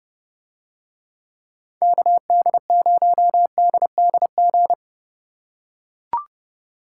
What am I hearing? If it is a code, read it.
Morse code, heard as a keyed tone, about 29 wpm, 700 Hz. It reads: KD0DDG